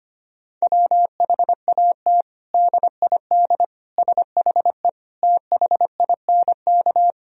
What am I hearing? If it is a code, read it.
Morse code, heard as a keyed tone, about 25 wpm, 700 Hz. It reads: WHAT DID SHE THINK
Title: Sentences from Top 1000 Words - Review Part 4 - 25wpm